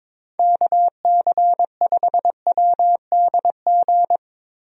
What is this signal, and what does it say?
Morse code, heard as a keyed tone, about 22 wpm, 700 Hz. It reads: KC5WDG
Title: USA Call Signs - 6 Characters - Repeated 3x - 22wpm